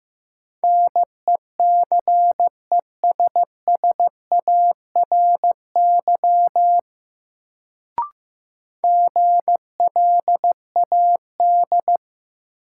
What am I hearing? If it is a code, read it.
Morse code, heard as a keyed tone, about 15 wpm, 700 Hz. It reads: NECESSARY GLAD
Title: Sentences from Top 800 Words - New Word Drill - 15wpm